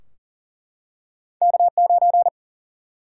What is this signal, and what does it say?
Morse code, heard as a keyed tone, about 42 wpm, 700 Hz. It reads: K9
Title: Single-Letter Single-Number 40wpm